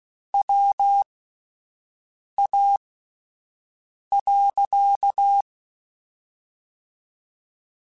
Morse code, heard as a keyed tone, about 16 wpm, spaced as for 5 wpm, 800 Hz. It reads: WA.
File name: (spaced) / QSOs Spaced-mp3s / WA7QOX 2 @10 sp